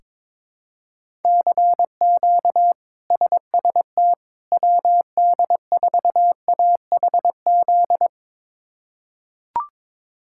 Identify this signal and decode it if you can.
Morse code, heard as a keyed tone, about 22 wpm, 700 Hz. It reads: CQ SST WD4AHZ
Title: K1USN SST QSO Practice - 22wpm